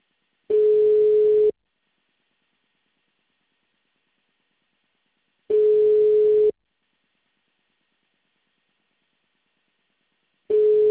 Ringback tone (Europe, 425 Hz).
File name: out- -2011- - - .5267